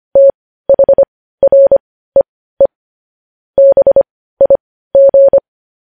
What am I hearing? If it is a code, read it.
Morse code, heard as a keyed tone, about 25 wpm, spaced as for 15 wpm, 550 Hz. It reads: THREE BIG
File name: podcast / News headlines-1x-25x15